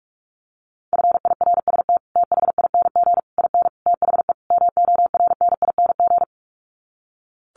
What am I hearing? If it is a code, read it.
Morse code, heard as a keyed tone, about 44 wpm, 700 Hz. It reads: FIRST THING IN THE MORNING
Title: Sets of 5 Words Encore2 45wpm